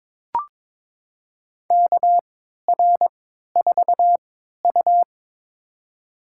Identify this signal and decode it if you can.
Morse code, heard as a keyed tone, about 23 wpm, spaced as for 13 wpm, 690 Hz. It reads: KR4U
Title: USA Call Signs - 4 Characters - 3x Character Spacing - 22wpm